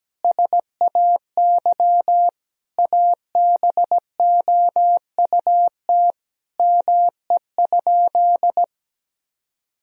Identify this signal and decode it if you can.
Morse code, heard as a keyed tone, about 17 wpm, 700 Hz. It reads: SAY ABOUT ME?
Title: Sentences from Top 1000 Words - Review Part 4 - 17wpm